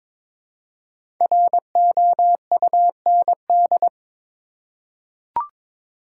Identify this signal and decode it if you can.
Morse code, heard as a keyed tone, about 22 wpm, 690 Hz. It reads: ROUND
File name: 5 Letter Words - First 3 Letters Given - 22wpm